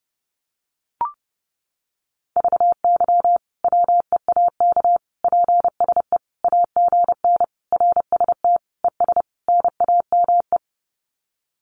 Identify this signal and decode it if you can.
Morse code, heard as a keyed tone, about 30 wpm, 700 Hz. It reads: VY WEAK PSE AGN RST ES NAME
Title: Common QSO Phrases 30wpm